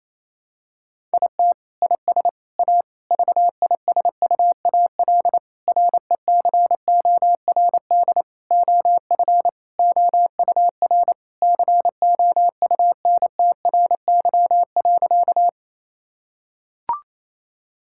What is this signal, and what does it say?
Morse code, heard as a keyed tone, about 28 wpm, 700 Hz. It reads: IT IS A VISUAL RECORD OF OUR COUNTRY.
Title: Sentences from Top 1000 Words - Review Part 6 - 28wpm